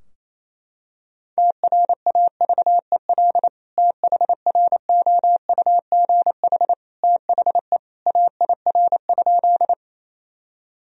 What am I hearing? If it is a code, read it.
Morse code, heard as a keyed tone, about 28 wpm, 700 Hz. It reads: TRAVEL THROUGH THE AIR?
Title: Sentences from Top 1000 Words - Review Part 2 - 28wpm